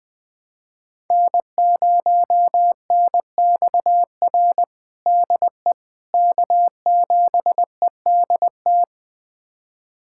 Morse code, heard as a keyed tone, about 21 wpm, 700 Hz. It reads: N0NXR DE K7EDT
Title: USA Callsign DE Callsign - 5 Characters - 20wpm